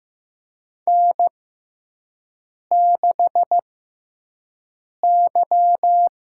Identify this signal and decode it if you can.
Morse code, heard as a keyed tone, about 15 wpm, spaced as for 5 wpm, 700 Hz. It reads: N6Y
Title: USA Call Signs - 5 Characters - 6x Character Spacing - 15wpm